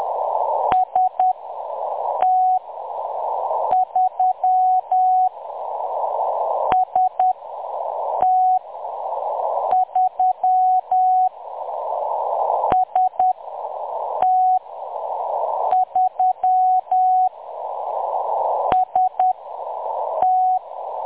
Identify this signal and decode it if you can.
Morse code, heard as a keyed tone, about 10 wpm, 740 Hz. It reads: S T 3 S T 3 S T 3 S T
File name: M23-ST3- -0858utc-10184khz